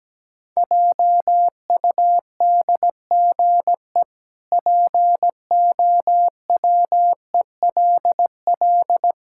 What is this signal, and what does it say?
Morse code, heard as a keyed tone, about 17 wpm, 700 Hz. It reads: JUDGE POWELL
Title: Pangram Challenge v2 - 17wpm